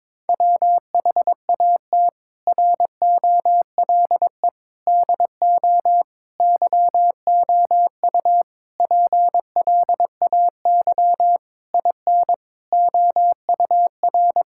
Morse code, heard as a keyed tone, about 22 wpm, 700 Hz. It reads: WHAT ROLE DO YOU PLAY IN OUR
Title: Sentences from Top 1000 Words - Review Part 6 - 22wpm